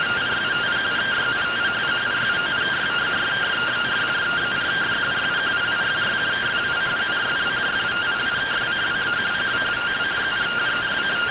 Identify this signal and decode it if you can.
Two-tone teleprinter signal, about 170 Hz shift.